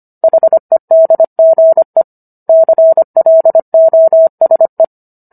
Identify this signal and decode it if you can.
Morse code, heard as a keyed tone, about 25 wpm, 650 Hz. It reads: HEDGE CLOSE